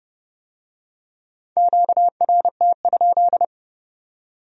Morse code, heard as a keyed tone, about 29 wpm, 700 Hz. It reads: QRT?